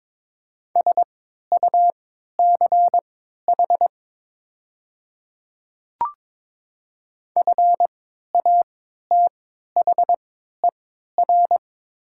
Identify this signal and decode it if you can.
Morse code, heard as a keyed tone, about 22 wpm, spaced as for 12 wpm, 690 Hz. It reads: SUCH FATHER
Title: 3x Character Spacing - Top 200 Words - 22wpm